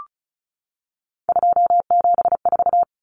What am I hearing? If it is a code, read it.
Morse code, heard as a keyed tone, about 34 wpm, 700 Hz. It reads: 274
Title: Numbers - 3-Digit 35wpm